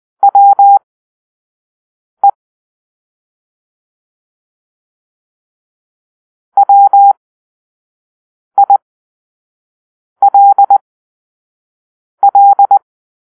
Morse code, heard as a keyed tone, about 20 wpm, spaced as for 5 wpm, 800 Hz. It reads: WE WILL
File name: arrl / Grp-5 Drill-3